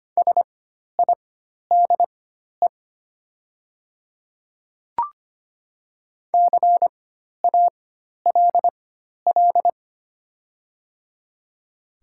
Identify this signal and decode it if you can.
Morse code, heard as a keyed tone, about 25 wpm, spaced as for 12 wpm, 690 Hz. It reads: SIDE CALL
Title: 4x Character Spacing - Top 100 Words - 25wpm